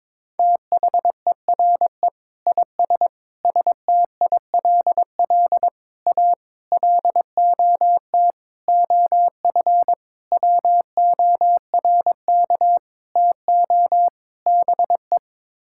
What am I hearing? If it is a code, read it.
Morse code, heard as a keyed tone, about 22 wpm, 700 Hz. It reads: THERE IS STILL A LOT OF WORK TO BE
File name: Sets of 9 Words - 22wpm